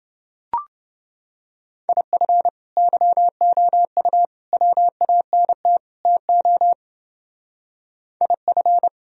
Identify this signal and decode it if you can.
Morse code, heard as a keyed tone, about 28 wpm, 700 Hz. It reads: IF YOU WANT TO IF